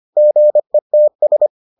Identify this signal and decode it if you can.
Morse code, heard as a keyed tone, about 25 wpm, 600 Hz. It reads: GETS